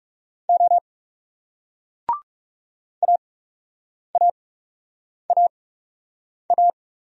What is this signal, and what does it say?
Morse code, heard as a keyed tone, about 40 wpm, spaced as for 8 wpm, 700 Hz. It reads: O AAAA